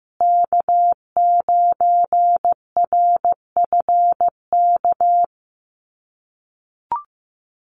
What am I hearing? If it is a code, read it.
Morse code, heard as a keyed tone, about 15 wpm, 700 Hz. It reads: K9RFK
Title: USA Call Signs Repeated 3x 15wpm